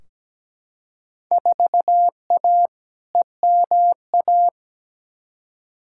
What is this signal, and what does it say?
Morse code, heard as a keyed tone, about 17 wpm, 700 Hz. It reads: 4A EMA
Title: Field Day Exchanges - 17wpm